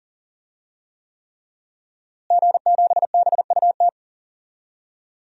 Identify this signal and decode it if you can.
Morse code, heard as a keyed tone, about 40 wpm, 700 Hz. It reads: G7BUT
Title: DXCC-100 Call Signs - 5 Characters - Repeated 3x - 40wpm